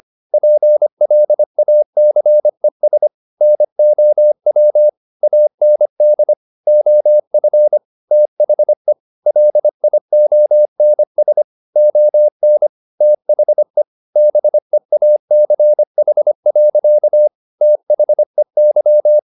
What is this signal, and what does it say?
Morse code, heard as a keyed tone, about 25 wpm, 600 Hz. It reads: PLACES NOW AND OF THE LIONS ON THE BEACH. THEY
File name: the old man and the sea part 1 25wpm